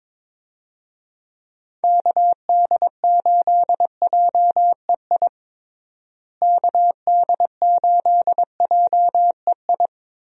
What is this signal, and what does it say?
Morse code, heard as a keyed tone, about 22 wpm, 700 Hz. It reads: KD8JEI KD8JEI
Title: USA Call Signs - 6 Characters - Repeated 3x - 22wpm